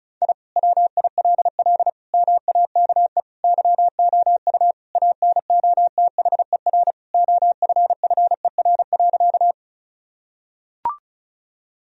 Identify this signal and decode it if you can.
Morse code, heard as a keyed tone, about 36 wpm, 700 Hz. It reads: I WILL MAKE YOU ANOTHER OFFER.
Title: Sentences from Top 1000 Words - Review Part 4 - 35wpm